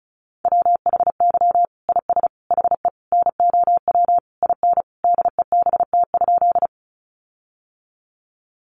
Morse code, heard as a keyed tone, about 34 wpm, 700 Hz. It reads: WHY IS HE NOW IN DEBT?
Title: Sentences from Top 1300 Words 35wpm